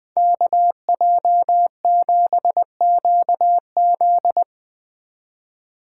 Morse code, heard as a keyed tone, about 23 wpm, 700 Hz. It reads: KJ7QZ
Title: Speed Racing - USA 2x2 Call Signs 20wpm